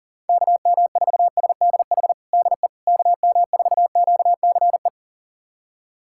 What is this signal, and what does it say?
Morse code, heard as a keyed tone, about 40 wpm, 700 Hz. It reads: KM4SDH DE KM4QCE
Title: USA Callsign DE Callsign 40wpm